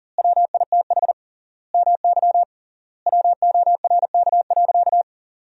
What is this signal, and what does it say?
Morse code, heard as a keyed tone, about 40 wpm, 700 Hz. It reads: WITH MY WORK.
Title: 3x Word-Spacing - Sentences from Top 300 Words - 40wpm